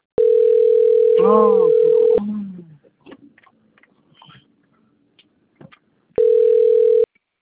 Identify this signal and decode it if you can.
Ringback tone (North America), 440 + 480 Hz, with speech.
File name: out- - - - - .93612